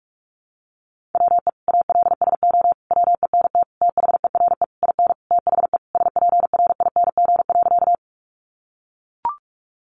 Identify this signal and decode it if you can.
Morse code, heard as a keyed tone, about 44 wpm, 700 Hz. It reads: WE ALSO WENT THERE IN THE SPRING.
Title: Sentences from Top 1200 Words 45wpm